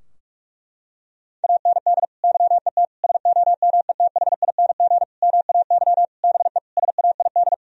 Morse code, heard as a keyed tone, about 46 wpm, 700 Hz. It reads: AND YET SOMETHING MAY BE SAID
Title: Sentences from Top 1000 Words - Review Part 3 - 45wpm